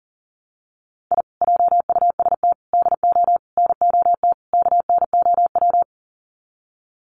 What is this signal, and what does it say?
Morse code, heard as a keyed tone, about 40 wpm, 700 Hz. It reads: I JUST DO NOT KNOW
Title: Sets of 5 Words Encore 40wpm